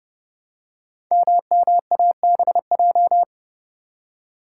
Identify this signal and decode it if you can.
Morse code, heard as a keyed tone, about 30 wpm, 700 Hz. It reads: MMABJ
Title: Code Groups - 5 Letter:Numbers - 30wpm